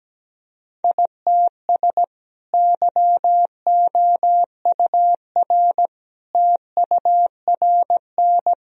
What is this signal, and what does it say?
Morse code, heard as a keyed tone, about 17 wpm, 700 Hz. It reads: ITS YOUR TURN
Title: Sentences from Top 1000 Words - Review Part 1 - 17wpm